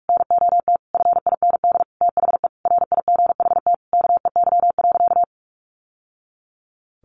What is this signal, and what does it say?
Morse code, heard as a keyed tone, about 44 wpm, 700 Hz. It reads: NOT FIND THE RIGHT KEY.
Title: Sentences from Top 600 Words 45wpm